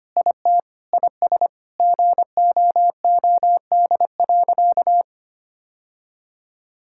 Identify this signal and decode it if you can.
Morse code, heard as a keyed tone, about 25 wpm, 700 Hz. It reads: IT IS GOOD.
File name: practice / 25wpm / Sentences with 3-Word Phrases - 25wpm